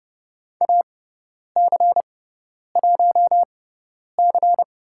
Morse code, heard as a keyed tone, about 30 wpm, spaced as for 10 wpm, 700 Hz. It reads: AC1C
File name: Lesson 16g - Letter c - Call Signs 30wpm